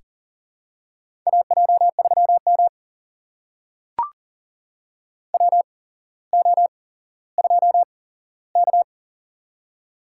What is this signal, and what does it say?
Morse code, heard as a keyed tone, about 40 wpm, spaced as for 17 wpm, 700 Hz. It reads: AJ3M WO2K
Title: USA Call Signs - 4 Characters - 8x Character Spacing - 40wpm